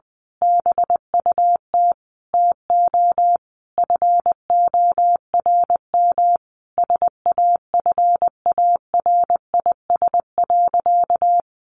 Morse code, heard as a keyed tone, about 20 wpm, 700 Hz. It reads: BUT TO FORM SAFARIS.
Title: A Sound Of Thunder 20wpm